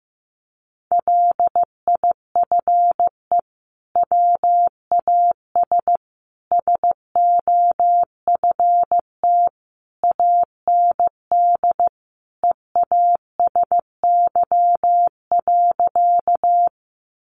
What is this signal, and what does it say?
Morse code, heard as a keyed tone, about 14 wpm, 700 Hz. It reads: LIFE WAS SOFT AND EASY.
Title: Sentences from Top 1500 Words 15wpm